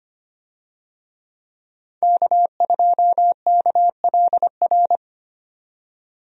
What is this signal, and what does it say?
Morse code, heard as a keyed tone, about 25 wpm, 700 Hz. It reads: K2KLR
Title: USA Call Signs - 5 Characters - 25wpm